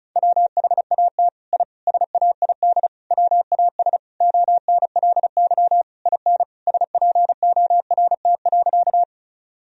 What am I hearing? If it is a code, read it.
Morse code, heard as a keyed tone, about 36 wpm, 700 Hz. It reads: WHAT I SAID WAS ONLY IN SPORT.
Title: Sentences from Top 1000 Words - Review Part 4 - 35wpm